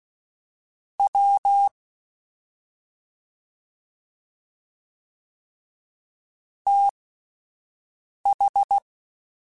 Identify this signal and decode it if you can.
Morse code, heard as a keyed tone, about 16 wpm, spaced as for 5 wpm, 800 Hz. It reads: W TH
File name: Misc / Misc-mp3s / Jumps @5